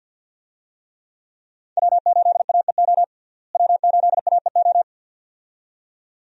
Morse code, heard as a keyed tone, about 47 wpm, 700 Hz. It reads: W8AEO W8AEO
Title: USA Call Signs - 5 Characters - Repeated 2x - 50wpm